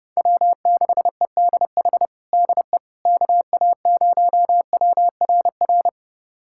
Morse code, heard as a keyed tone, about 30 wpm, 700 Hz. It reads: W6EDH DE KA0WRR
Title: USA Callsign DE Callsign 30wpm